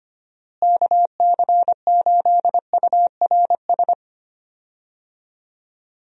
Morse code, heard as a keyed tone, about 25 wpm, 700 Hz. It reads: KC8URS
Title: USA Call Signs - 6 Characters - Repeated 3x - 25wpm